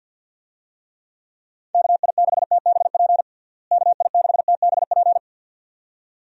Morse code, heard as a keyed tone, about 50 wpm, 700 Hz. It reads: KI6TBP KI6TBP
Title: USA Call Signs - 6 Characters - Repeated 2x - 50wpm